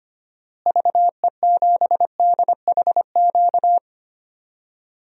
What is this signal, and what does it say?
Morse code, heard as a keyed tone, about 25 wpm, 700 Hz. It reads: VE7DHQ
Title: DXCC-100 Call Signs - 6 Characters - Repeated 3x - 25wpm